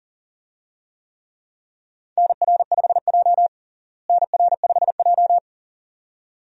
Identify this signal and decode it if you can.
Morse code, heard as a keyed tone, about 40 wpm, 700 Hz. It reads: NR5J NR5J
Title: USA Call Signs - 4 Characters - Repeated 2x - 40wpm